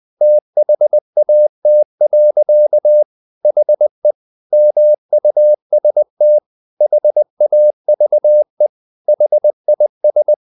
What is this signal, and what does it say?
Morse code, heard as a keyed tone, about 20 wpm, 600 Hz. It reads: THAT. HE MUST HAVE HIS
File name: the old man and the sea / the old man and the sea part 1 20wpm